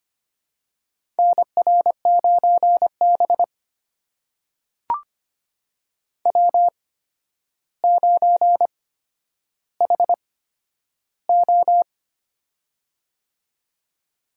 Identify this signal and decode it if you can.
Morse code, heard as a keyed tone, about 25 wpm, spaced as for 11 wpm, 700 Hz. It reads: NR9B W9HO
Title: USA Call Signs - 4 Characters - 8x Character Spacing - 25wpm